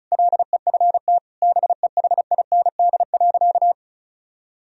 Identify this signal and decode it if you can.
Morse code, heard as a keyed tone, about 36 wpm, 700 Hz. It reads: LEFT BEHIND.
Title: Sentences from Top 1000 Words - Review Part 1 - 35wpm